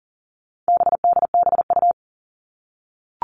Morse code, heard as a keyed tone, about 40 wpm, 700 Hz. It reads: BDBU
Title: All Cut Numbers - 4-Digit 40wpm